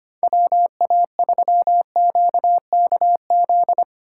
Morse code, heard as a keyed tone, about 31 wpm, 700 Hz. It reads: WA3QKZ
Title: Speed Racing - USA 2x3 Call Signs 25wpm